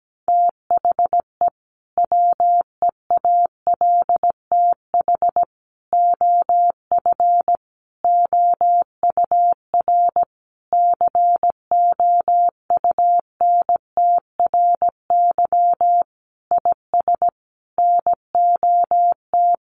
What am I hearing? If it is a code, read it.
Morse code, heard as a keyed tone, about 17 wpm, 700 Hz. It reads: THE WEALTH OF OUR COUNTRY IS NOT
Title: Sentences from Top 1200 Words 17wpm